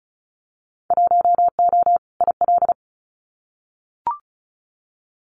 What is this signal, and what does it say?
Morse code, heard as a keyed tone, about 36 wpm, 700 Hz. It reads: 1O IL
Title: Winter Field Day Exchanges 35wpm